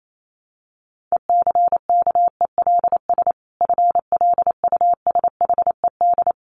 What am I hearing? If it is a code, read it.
Morse code, heard as a keyed tone, about 28 wpm, 700 Hz. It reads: ECKELS FLUSHED
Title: A Sound Of Thunder 28wpm